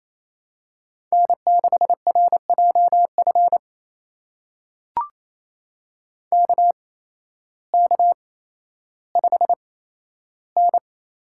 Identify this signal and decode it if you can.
Morse code, heard as a keyed tone, about 28 wpm, spaced as for 13 wpm, 700 Hz. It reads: N6RJF KK5N
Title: USA Call Signs - 5 Characters - 8x Character Spacing - 28wpm